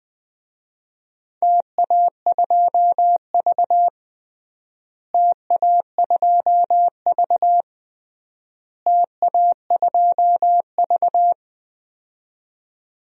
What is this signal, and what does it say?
Morse code, heard as a keyed tone, about 20 wpm, 700 Hz. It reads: TA2V TA2V TA2V
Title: DXCC-100 Call Signs - 4 Characters - Repeated 3x - 20wpm